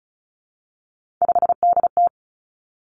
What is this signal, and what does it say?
Morse code, heard as a keyed tone, about 36 wpm, 700 Hz. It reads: 5DT
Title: Three Challenging Letter-Number Combinations 35wpm